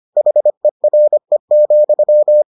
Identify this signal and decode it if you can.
Morse code, heard as a keyed tone, about 25 wpm, 600 Hz. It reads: HERE,